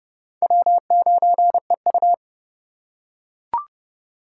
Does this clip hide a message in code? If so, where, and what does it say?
Yes: Morse code from 0.4 to 2.1 s, heard as a keyed tone, about 29 wpm, 700 Hz. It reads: W9EU